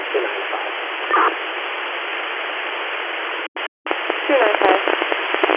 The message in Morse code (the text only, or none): none